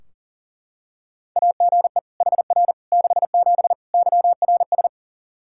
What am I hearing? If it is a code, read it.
Morse code, heard as a keyed tone, about 37 wpm, 700 Hz. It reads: AGE HR 67 YRS
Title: Practice QSO - Exchange 3 - More Info - 40wpm